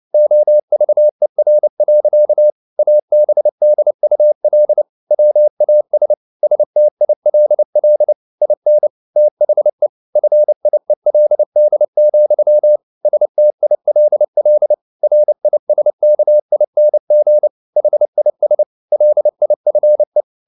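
Morse code, heard as a keyed tone, about 29 wpm, 600 Hz. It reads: OVER. ABDUL WAS STILL IN THE FIELD, STILL RISKING HIS LIFE
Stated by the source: Morse code at 29 wpm, sending Ken Follett's novel Never in English, from CW med Gnister.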